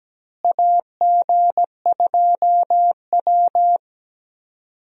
Morse code, heard as a keyed tone, about 17 wpm, 700 Hz. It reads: AG2W
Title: USA Call Signs - 4 Characters - Repeated 3x - 17wpm